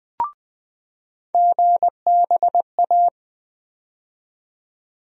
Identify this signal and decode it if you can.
Morse code, heard as a keyed tone, about 20 wpm, 700 Hz. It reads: GBA
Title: USA Call Sign Postfixes - 3 Letters 20wpm